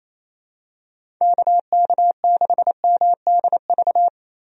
Morse code, heard as a keyed tone, about 28 wpm, 700 Hz. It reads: KK6MDV
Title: USA Call Signs - 6 Characters - Repeated 3x - 28wpm